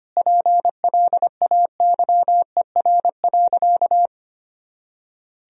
Morse code, heard as a keyed tone, about 25 wpm, 700 Hz. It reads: PLAYER.
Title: Sentences from Top 1000 Words - Review Part 4 - 25wpm